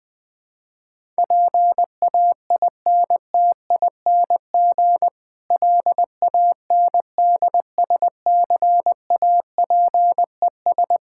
Morse code, heard as a keyed tone, about 21 wpm, 700 Hz. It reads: PAINTING LANDSCAPES
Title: Difficult Sentences with Hints - 20wpm